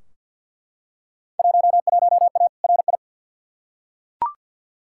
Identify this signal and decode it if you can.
Morse code, heard as a keyed tone, about 46 wpm, 700 Hz. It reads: 11A RI
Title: Field Day Exchanges Encore - 50wpm